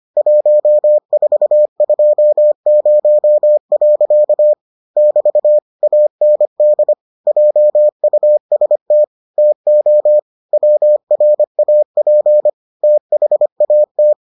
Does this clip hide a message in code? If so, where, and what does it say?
Yes: Morse code from 0.2 to 14.1 s, heard as a keyed tone, about 25 wpm, 600 Hz. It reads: 1420. <BT> AND JUST TO WRAP THAT